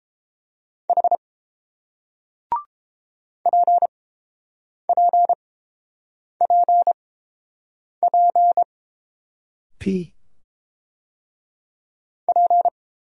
Morse code, heard as a keyed tone, about 30 wpm, spaced as for 7 wpm, 700 Hz. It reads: H PPPP P